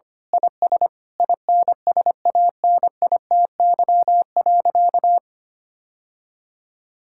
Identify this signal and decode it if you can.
Morse code, heard as a keyed tone, about 25 wpm, 700 Hz. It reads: IS INSANITY.
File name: practice / 25wpm / Sentences with 9-Word Phrases - 25wpm